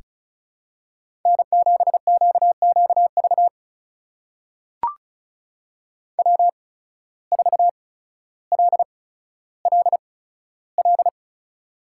Morse code, heard as a keyed tone, about 36 wpm, spaced as for 15 wpm, 700 Hz. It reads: N7QQV W4LLL
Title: USA Call Signs - 5 Characters - 8x Character Spacing - 35wpm